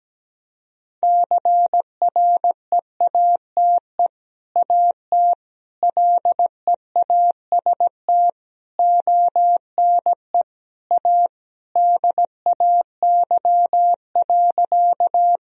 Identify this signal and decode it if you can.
Morse code, heard as a keyed tone, about 17 wpm, 700 Hz. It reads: CREATE AT LEAST ONE A DAY.
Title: Sentences from Top 1000 Words - Review Part 3 - 17wpm